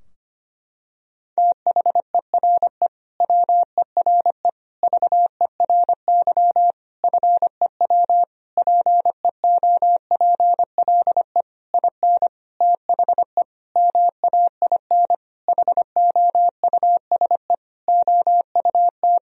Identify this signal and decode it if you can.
Morse code, heard as a keyed tone, about 25 wpm, 700 Hz. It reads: THERE WERE VERY FEW PEOPLE IN THE MAIN HOUSE OUT
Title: Sentences from Top 1000 Words - Review Part 2 - 25wpm